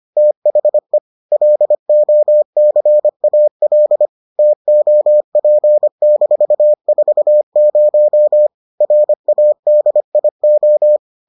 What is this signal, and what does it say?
Morse code, heard as a keyed tone, about 25 wpm, 600 Hz. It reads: THE LOCAL TOP-40 RADIO